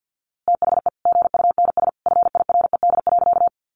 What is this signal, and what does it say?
Morse code, heard as a keyed tone, about 50 wpm, 700 Hz. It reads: THE GUNS FIRED.
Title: A Sound Of Thunder 50wpm